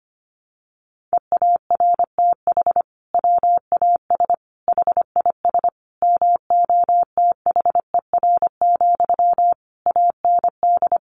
Morse code, heard as a keyed tone, about 25 wpm, 700 Hz. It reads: EARTH WAS HIS MOTHER, AND
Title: Sentences from Top 1200 Words 25wpm